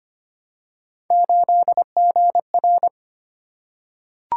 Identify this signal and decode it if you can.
Morse code, heard as a keyed tone, about 25 wpm, 700 Hz. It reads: 8GR
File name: Three Letter-Number Combinations - 25wpm